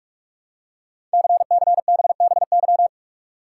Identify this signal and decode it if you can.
Morse code, heard as a keyed tone, about 44 wpm, 700 Hz. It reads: CCBBY
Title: Code Groups - 5 Letters - 45wpm